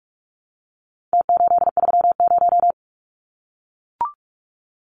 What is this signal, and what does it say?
Morse code, heard as a keyed tone, about 44 wpm, 700 Hz. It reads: T830
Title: Numbers - Contest Style - 4-Digit 45wpm